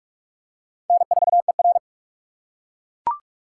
Morse code, heard as a keyed tone, about 41 wpm, 700 Hz. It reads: N4ER
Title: USA Call Signs - 4 Characters - 4x Character Spacing - 45wpm